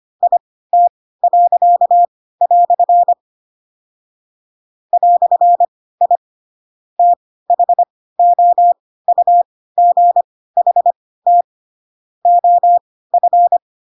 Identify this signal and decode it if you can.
Morse code, heard as a keyed tone, about 25 wpm, spaced as for 16 wpm, 700 Hz. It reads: IT." "I THOUGHT OF
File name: Pinocchio / Pinocchio ch2 10 25 ik1hge